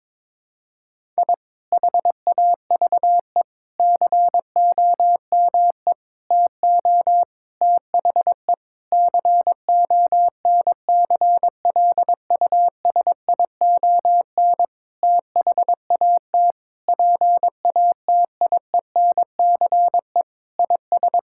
Morse code, heard as a keyed tone, about 22 wpm, 700 Hz. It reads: I HAVE COME TO THE CONCLUSION THAT PATIENCE IS
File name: Sentences with 7-Word Phrases - 22wpm